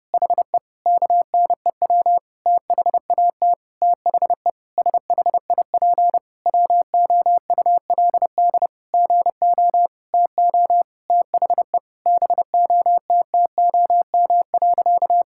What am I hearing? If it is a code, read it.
Morse code, heard as a keyed tone, about 30 wpm, 700 Hz. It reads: HE KNEW THAT THE SHIP WOULD GO TO THE BOTTOM.
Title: Sentences from Top 1000 Words - Review Part 6 - 30wpm